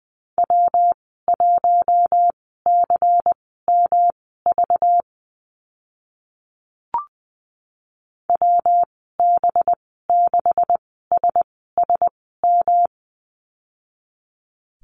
Morse code, heard as a keyed tone, about 19 wpm, spaced as for 14 wpm, 700 Hz. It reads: W1CMV WB6SSM
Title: 193 Lesson 31g - Letter v - Call Signs 20wpm